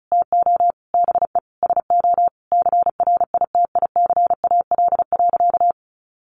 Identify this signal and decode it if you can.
Morse code, heard as a keyed tone, about 34 wpm, 700 Hz. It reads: TO BE SO CRITICAL.
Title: Sentences from Top 1100 Words 35wpm